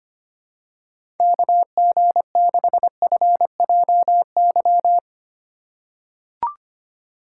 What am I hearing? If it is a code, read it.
Morse code, heard as a keyed tone, about 25 wpm, 700 Hz. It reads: KG6FJY
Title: USA Call Signs - 6 Characters - 2x Character Spacing - 25wpm